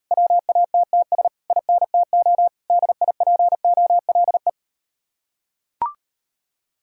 Morse code, heard as a keyed tone, about 39 wpm, 700 Hz. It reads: WATTS INTO DIPOLE